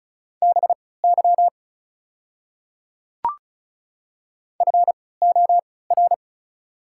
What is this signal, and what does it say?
Morse code, heard as a keyed tone, about 36 wpm, spaced as for 20 wpm, 690 Hz. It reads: BY FOR
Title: 3x Character Spacing - Top 100 Words - 35wpm